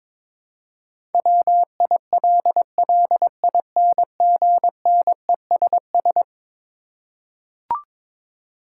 Morse code, heard as a keyed tone, about 21 wpm, 700 Hz. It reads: WILLINGNESS
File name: practice / 22wpm / Eleven Letter Words - 22wpm